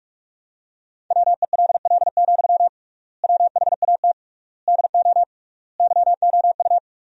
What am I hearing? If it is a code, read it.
Morse code, heard as a keyed tone, about 43 wpm, 700 Hz. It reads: WELL, WHAT DO YOU
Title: 3x Word-Spacing - Sentences from Top 200 Words - 45wpm